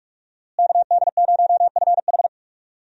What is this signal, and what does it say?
Morse code, heard as a keyed tone, about 42 wpm, 700 Hz. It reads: KD0FH
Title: USA Call Signs - 5 Characters - Repeated 2x - 45wpm